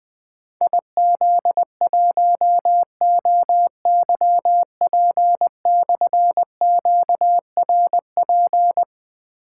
Morse code, heard as a keyed tone, about 20 wpm, 700 Hz. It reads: IZ1OYP/QRP
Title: Difficult Call Signs 20wpm